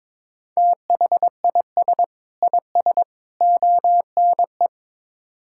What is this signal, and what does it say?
Morse code, heard as a keyed tone, about 22 wpm, 700 Hz. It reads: THIS IS ONE
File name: practice / 22wpm / Build A Phrase - 5-Word Phrases - 22wpm